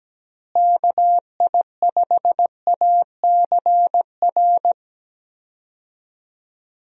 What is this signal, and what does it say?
Morse code, heard as a keyed tone, about 17 wpm, 700 Hz. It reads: KI5ACR
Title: USA Call Signs - 6 Characters - Repeated 2x - 17wpm